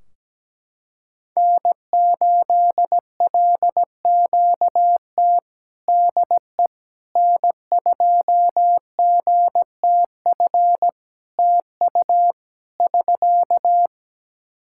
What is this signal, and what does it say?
Morse code, heard as a keyed tone, about 17 wpm, 700 Hz. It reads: N8LQT DE N2GTF TU <SK>
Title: K1USN SST QSO Practice 17wpm